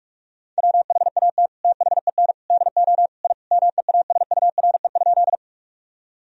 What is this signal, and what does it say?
Morse code, heard as a keyed tone, about 46 wpm, 700 Hz. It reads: WHAT THEN DO I MEASURE?
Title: Sentences from Top 1000 Words - Review Part 2 - 45wpm